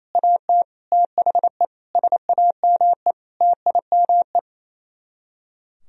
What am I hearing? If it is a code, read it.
Morse code, heard as a keyed tone, about 27 wpm, 700 Hz. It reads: AT THE SAME TIME
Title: Speed-Racing - Sets of 5 Words 28wpm